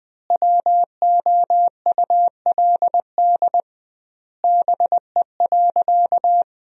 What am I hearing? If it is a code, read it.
Morse code, heard as a keyed tone, about 20 wpm, 700 Hz. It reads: WOULD BE.
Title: 2x Word-Spacing - Sentences from Top 200 Words - 20wpm